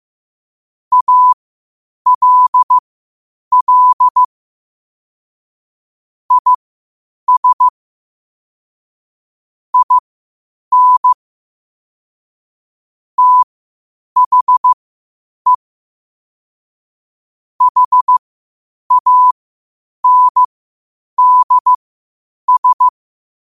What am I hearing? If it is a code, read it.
Morse code, heard as a keyed tone, about 15 wpm, spaced as for 8 wpm, 1000 Hz. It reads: ALL IS IN THE HANDS
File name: qotdfiles / 8wpm